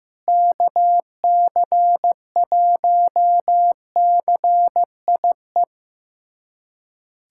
Morse code, heard as a keyed tone, about 15 wpm, 700 Hz. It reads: KC1CIE
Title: USA Callsign DE Callsign - 6 Characters - 15wpm